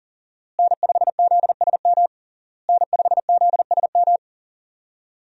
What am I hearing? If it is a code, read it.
Morse code, heard as a keyed tone, about 40 wpm, 700 Hz. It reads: N5ZSM N5ZSM
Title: USA Call Signs - 5 Characters - Repeated 3x - 40wpm